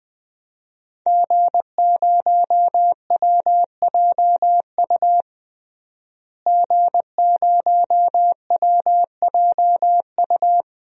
Morse code, heard as a keyed tone, about 20 wpm, 700 Hz. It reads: G0WJU G0WJU
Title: DXCC-100 Call Signs - 5 Characters - Repeated 3x - 20wpm